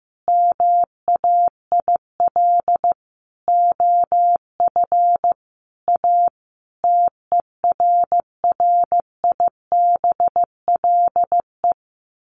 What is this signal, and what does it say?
Morse code, heard as a keyed tone, about 15 wpm, 700 Hz. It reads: MAIL OF A TERRIBLE
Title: A Sound Of Thunder 15wpm